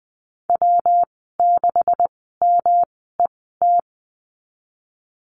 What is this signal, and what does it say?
Morse code, heard as a keyed tone, about 20 wpm, spaced as for 14 wpm, 700 Hz. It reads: W6MET
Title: 161 Lesson 26g - Letter g - Call Signs 20wpm